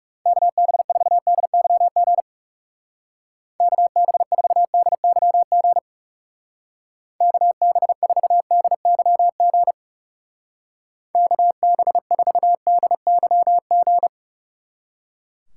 Morse code, heard as a keyed tone, about 36 wpm, 700 Hz. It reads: KB4DYG KB4DYG KB4DYG KB4DYG
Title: Speed Racing - USA 2x3 Call Signs 30wpm